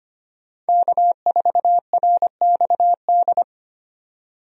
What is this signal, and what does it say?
Morse code, heard as a keyed tone, about 25 wpm, 700 Hz. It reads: K4RXD
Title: USA Call Signs - 5 Characters - 25wpm